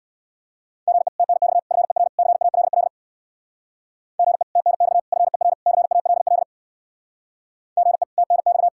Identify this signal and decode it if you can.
Morse code, heard as a keyed tone, about 66 wpm, 700 Hz. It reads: GE TNX FER CALL GE TNX FER CALL GE TNX